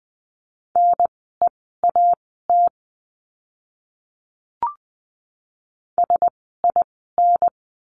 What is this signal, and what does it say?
Morse code, heard as a keyed tone, about 20 wpm, spaced as for 14 wpm, 700 Hz. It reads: NEAT SIN
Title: 036 Lesson 7f - Letter s - Words 20wpm